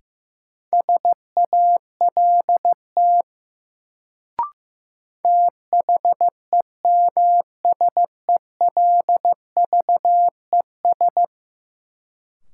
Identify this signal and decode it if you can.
Morse code, heard as a keyed tone, about 15 wpm, 700 Hz. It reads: SALT THEMSELVES
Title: Sentences from Top 800 Words - New Word Drill - 15wpm